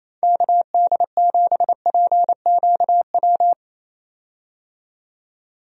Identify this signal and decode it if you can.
Morse code, heard as a keyed tone, about 27 wpm, 700 Hz. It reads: KD7PQW
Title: USA Call Signs - 6 Characters - 28wpm